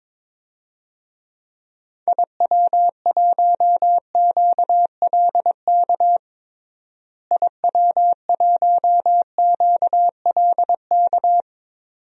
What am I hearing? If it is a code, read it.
Morse code, heard as a keyed tone, about 21 wpm, 700 Hz. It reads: IW1QLK IW1QLK
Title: DXCC-100 Call Signs - 6 Characters - Repeated 2x - 22wpm